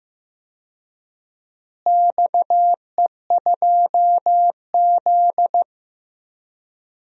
Morse code, heard as a keyed tone, about 15 wpm, 700 Hz. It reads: XE2Z
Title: DXCC-100 Call Signs - 4 Characters - Repeated 2x - 15wpm